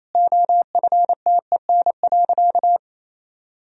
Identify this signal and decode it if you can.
Morse code, heard as a keyed tone, about 28 wpm, 700 Hz. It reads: OFTEN.
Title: Sentences from Top 1000 Words - Review Part 3 - 28wpm